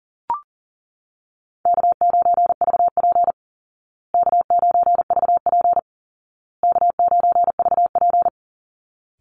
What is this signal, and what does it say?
Morse code, heard as a keyed tone, about 40 wpm, 700 Hz. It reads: K9VP K9VP K9VP
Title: Call Signs Repeated 2x - 40wpm